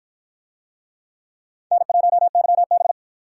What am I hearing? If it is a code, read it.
Morse code, heard as a keyed tone, about 53 wpm, 700 Hz. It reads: N1YB